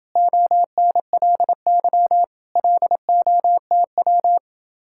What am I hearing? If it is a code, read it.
Morse code, heard as a keyed tone, about 28 wpm, 700 Hz. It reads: ONLY LOTW